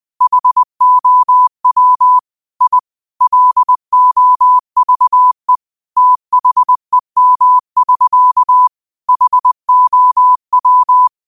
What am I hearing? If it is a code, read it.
Morse code, heard as a keyed tone, about 20 wpm, 1000 Hz. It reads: HOW I LOVE THEM<SK> HOW
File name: qotdfiles / 20wpm